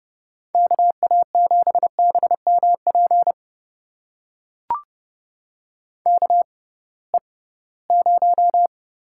Morse code, heard as a keyed tone, about 32 wpm, spaced as for 20 wpm, 700 Hz. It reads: KA7BMP KE0
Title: USA Call Signs - 6 Characters - 6x Character Spacing - 30wpm